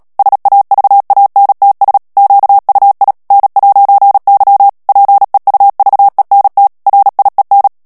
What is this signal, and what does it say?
Morse code, heard as a keyed tone, about 37 wpm, 800 Hz. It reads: SAVANTS QUI N'Y PEUVENT RIEN
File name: CW / FICHIERS-MP3 / FRANCAIS TEXTE CLAIR 2 VITESSE 35 MOTS MINUTES